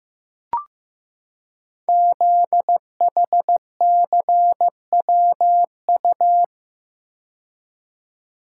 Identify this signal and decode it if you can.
Morse code, heard as a keyed tone, about 15 wpm, 700 Hz. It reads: ZHCWU
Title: Code Groups - 5 Letters - 15wpm